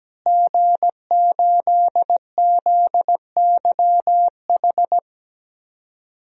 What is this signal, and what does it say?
Morse code, heard as a keyed tone, about 17 wpm, 700 Hz. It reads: G8ZYH
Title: DXCC-100 Call Signs - 5 Characters - Repeated 3x - 17wpm